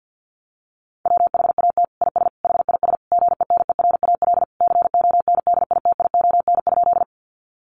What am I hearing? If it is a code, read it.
Morse code, heard as a keyed tone, about 50 wpm, 700 Hz. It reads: WHAT IS HIS GENERAL CONDITION?
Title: Sentences from Top 1000 Words 50wpm